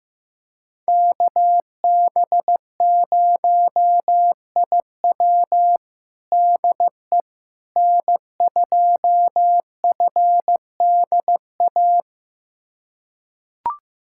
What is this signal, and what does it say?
Morse code, heard as a keyed tone, about 16 wpm, 700 Hz. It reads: KB0IW DE N2FDA
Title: USA Callsign DE Callsign 15wpm